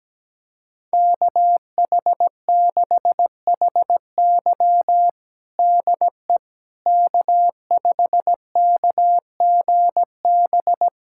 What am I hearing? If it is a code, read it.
Morse code, heard as a keyed tone, about 17 wpm, 700 Hz. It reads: KH6HY DE K5KGB
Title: USA Callsign DE Callsign - 5 Characters - 17wpm